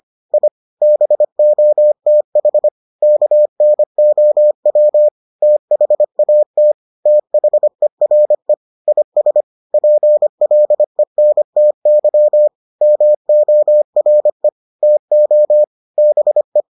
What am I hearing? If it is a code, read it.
Morse code, heard as a keyed tone, about 25 wpm, 600 Hz. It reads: I BOTH KNOW THAT THERE IS PLENTY MORE TO BE